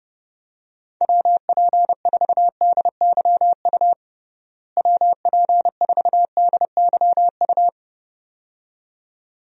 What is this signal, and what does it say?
Morse code, heard as a keyed tone, about 30 wpm, 700 Hz. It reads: WP4DYU WP4DYU
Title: USA Call Signs - 6 Characters - Repeated 2x - 30wpm